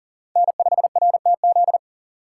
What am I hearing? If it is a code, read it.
Morse code, heard as a keyed tone, about 40 wpm, 700 Hz. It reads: N5RTZ
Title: USA Call Signs - 5 Characters - 3x Character Spacing - 40wpm